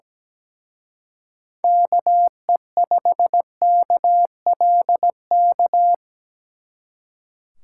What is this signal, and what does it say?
Morse code, heard as a keyed tone, about 20 wpm, 700 Hz. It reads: KE5KLK
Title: Speed Racing - USA 2x3 Call Signs 17wpm